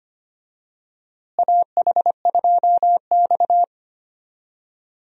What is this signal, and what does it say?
Morse code, heard as a keyed tone, about 25 wpm, 700 Hz. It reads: AH2X
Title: USA Call Signs - 4 Characters - Repeated 2x - 25wpm